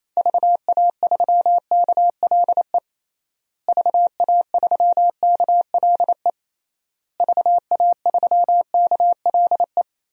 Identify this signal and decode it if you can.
Morse code, heard as a keyed tone, about 28 wpm, 700 Hz. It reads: VA3KLE VA3KLE VA3KLE
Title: DXCC-100 Call Signs - 6 Characters - Repeated 3x - 28wpm